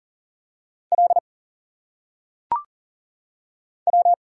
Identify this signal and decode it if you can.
Morse code, heard as a keyed tone, about 40 wpm, 700 Hz. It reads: L W
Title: Single Letters - Phonetic Pronunciation 40wpm